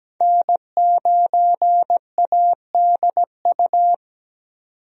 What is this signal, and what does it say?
Morse code, heard as a keyed tone, about 17 wpm, 700 Hz. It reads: N9ADU
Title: USA Call Signs - 5 Characters - 17wpm